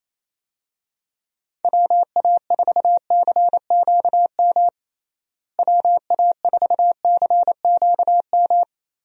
Morse code, heard as a keyed tone, about 28 wpm, 700 Hz. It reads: WA4CQM WA4CQM
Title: USA Call Signs - 6 Characters - Repeated 2x - 28wpm